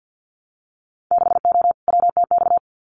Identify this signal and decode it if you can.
Morse code, heard as a keyed tone, about 50 wpm, 700 Hz. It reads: 6O WTX
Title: Winter Field Day Exchanges Encore 50wpm